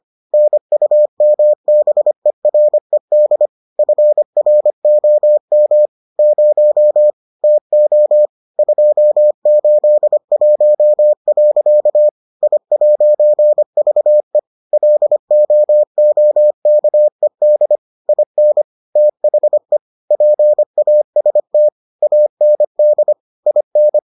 Morse code, heard as a keyed tone, about 25 wpm, 600 Hz. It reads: NUMBERED FROM 0 TO 281. I'VE LOOKED IN THE PAST AND IN